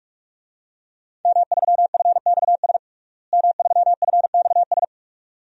Morse code, heard as a keyed tone, about 46 wpm, 700 Hz. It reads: M3FXS M3FXS
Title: DXCC-100 Call Signs - 5 Characters - Repeated 2x - 45wpm